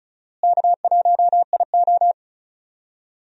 Morse code, heard as a keyed tone, about 36 wpm, 700 Hz. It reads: K1IO
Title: USA Call Signs - 4 Characters - Repeated 2x - 35wpm